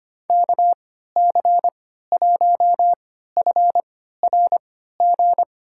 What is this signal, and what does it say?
Morse code, heard as a keyed tone, about 25 wpm, spaced as for 14 wpm, 700 Hz. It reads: KC1FRG
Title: USA Call Signs - 6 Characters - 3x Character Spacing - 25wpm